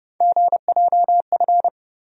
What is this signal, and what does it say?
Morse code, heard as a keyed tone, about 30 wpm, 700 Hz. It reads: GJF
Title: USA Call Sign Postfixes - 3 Letters 30wpm